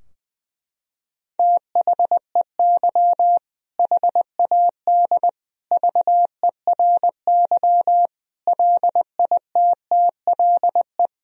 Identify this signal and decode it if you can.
Morse code, heard as a keyed tone, about 20 wpm, 700 Hz. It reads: THEY HAD VERY LITTLE
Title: Sentences from Top 1000 Words - Review Part 2 - 20wpm